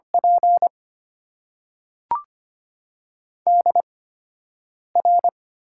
Morse code, heard as a keyed tone, about 25 wpm, spaced as for 7 wpm, 700 Hz. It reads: P DR
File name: Top 90 QSO Elements - 8x Character Spacing - 25wpm